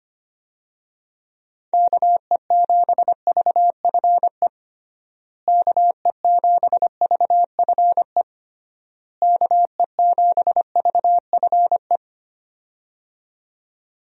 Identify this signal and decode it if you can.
Morse code, heard as a keyed tone, about 25 wpm, 700 Hz. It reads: KE7VFE KE7VFE KE7VFE
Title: USA Call Signs - 6 Characters - Repeated 3x - 25wpm